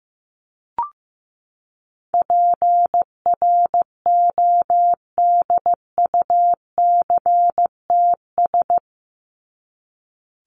Morse code, heard as a keyed tone, about 15 wpm, 700 Hz. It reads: PRODUCTS
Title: ES:S Suffix - Top 100 Words with Base Words 15wpm